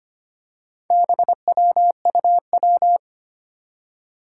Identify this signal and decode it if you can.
Morse code, heard as a keyed tone, about 25 wpm, 700 Hz. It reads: BWUW
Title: Four Challenging Letter-Number Combinations 25wpm